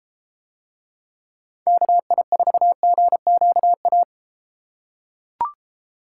Morse code, heard as a keyed tone, about 31 wpm, 700 Hz. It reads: KI4GQA